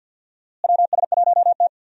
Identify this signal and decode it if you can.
Morse code, heard as a keyed tone, about 47 wpm, 700 Hz. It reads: WS1T
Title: USA Call Signs - 4 Characters - Repeated 3x - 50wpm